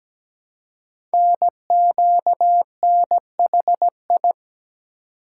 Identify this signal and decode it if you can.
Morse code, heard as a keyed tone, about 17 wpm, 700 Hz. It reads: NQNHI
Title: Code Groups - 5 Letters - 17wpm